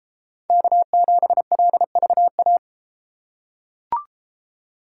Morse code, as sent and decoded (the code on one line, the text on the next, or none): -.- --... .-.. ...- .-
K7LVA